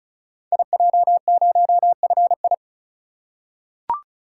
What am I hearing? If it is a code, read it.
Morse code, heard as a keyed tone, about 36 wpm, 700 Hz. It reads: IJ0FI
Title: Code Groups - 5 Letter:Numbers - 35wpm